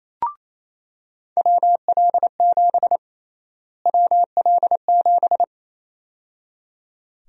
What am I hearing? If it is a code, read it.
Morse code, heard as a keyed tone, about 28 wpm, 700 Hz. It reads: WL7 WL7
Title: USA Call Sign Prefixes - Repeated 2x - 28wpm